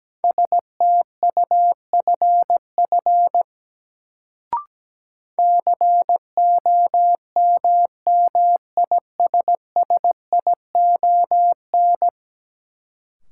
Morse code, heard as a keyed tone, about 17 wpm, 700 Hz. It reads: STUFF COMMISSION
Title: Sentences from Top 1400 Words - New Word Drill Alt - 17wpm